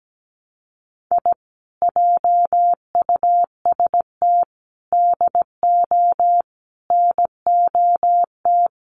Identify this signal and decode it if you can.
Morse code, heard as a keyed tone, about 17 wpm, 700 Hz. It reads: I JUST DO NOT
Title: Sets of 5 Words Encore 17wpm